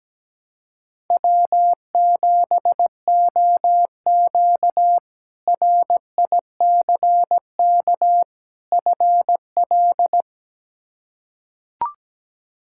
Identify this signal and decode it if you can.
Morse code, heard as a keyed tone, about 17 wpm, 700 Hz. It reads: W7OQ RICK FL
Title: K1USN SST QSO Practice - 17wpm